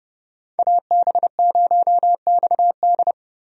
Morse code, heard as a keyed tone, about 30 wpm, 700 Hz. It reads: AB0XD
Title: USA Call Signs - 5 Characters - 30wpm